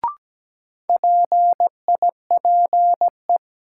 Morse code, heard as a keyed tone, about 17 wpm, 700 Hz. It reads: PIPE
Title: Sentences from Top 1500 Words - New Word Drill - 17wpm